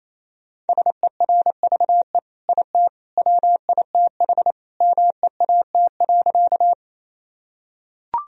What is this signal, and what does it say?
Morse code, heard as a keyed tone, about 27 wpm, 700 Hz. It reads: SERVE IT WITH MEAT.
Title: Sentences from Top 1000 Words - Review Part 3 - 28wpm